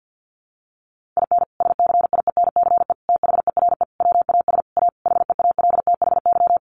Morse code, heard as a keyed tone, about 50 wpm, 700 Hz. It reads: IN SCIENCE THERE WAS A HEALTHY
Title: Sentences from Top 1400 Words 50wpm